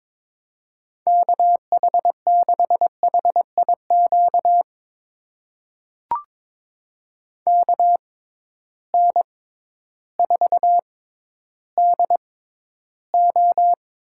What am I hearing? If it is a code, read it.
Morse code, heard as a keyed tone, about 22 wpm, spaced as for 12 wpm, 700 Hz. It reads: KH6HIQ KN4DO
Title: USA Call Signs - 6 Characters - 6x Character Spacing - 22wpm